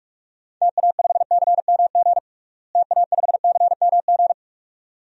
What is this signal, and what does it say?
Morse code, heard as a keyed tone, about 45 wpm, 700 Hz. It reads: TA5CMG TA5CMG
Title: DXCC-100 Call Signs - 6 Characters - Repeated 2x - 45wpm